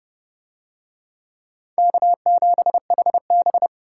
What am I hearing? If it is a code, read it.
Morse code, heard as a keyed tone, about 27 wpm, 700 Hz. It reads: K7HB